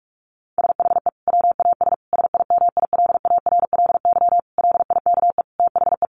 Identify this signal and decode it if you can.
Morse code, heard as a keyed tone, about 44 wpm, 700 Hz. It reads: SHE WAS SIMILARLY LIKE THE
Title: Sentences from Top 1400 Words 45wpm